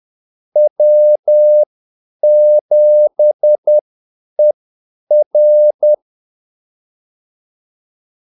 Morse code, heard as a keyed tone, about 10 wpm, spaced as for 8 wpm, 600 Hz. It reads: W7ER
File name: Callsigns-10-8wpm